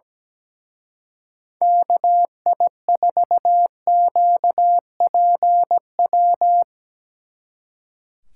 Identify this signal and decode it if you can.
Morse code, heard as a keyed tone, about 18 wpm, 700 Hz. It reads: KI4QPW
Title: Speed Racing - USA Call Signs 17wpm